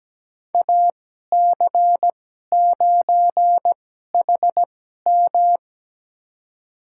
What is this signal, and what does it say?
Morse code, heard as a keyed tone, about 17 wpm, spaced as for 12 wpm, 700 Hz. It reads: AC9HM
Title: USA Call Signs - 2x Character Spacing - 17wpm